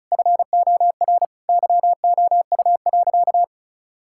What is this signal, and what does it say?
Morse code, heard as a keyed tone, about 36 wpm, 700 Hz. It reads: FOR YOU.
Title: Sentences from Top 1000 Words - Review Part 6 - 35wpm